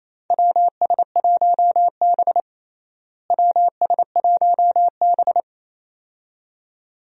Morse code, heard as a keyed tone, about 28 wpm, 700 Hz. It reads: WS1B WS1B
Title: USA Call Signs - 4 Characters - Repeated 2x - 28wpm